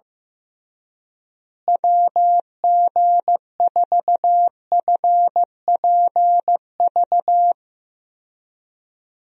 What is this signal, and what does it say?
Morse code, heard as a keyed tone, about 15 wpm, 700 Hz. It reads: WG4FPV
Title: USA Call Signs - 6 Characters - Repeated 2x - 15wpm